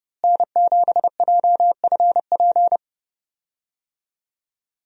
Morse code, heard as a keyed tone, about 29 wpm, 700 Hz. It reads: N7JFP
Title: USA Call Signs - 5 Characters - Repeated 3x - 30wpm